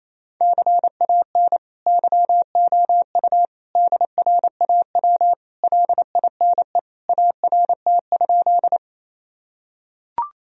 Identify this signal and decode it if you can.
Morse code, heard as a keyed tone, about 28 wpm, 700 Hz. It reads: CAN YOU DRAW LINE ART?
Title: Sentences from Top 1000 Words - Review Part 3 - 28wpm